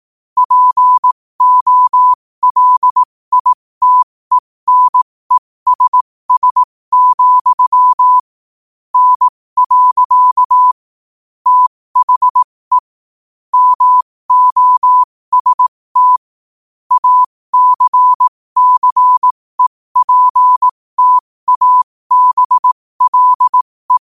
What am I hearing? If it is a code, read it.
Morse code, heard as a keyed tone, about 18 wpm, spaced as for 15 wpm, 1000 Hz. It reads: POLITENESS, N. THE MOST ACCEPTABLE
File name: qotdfiles / 15wpm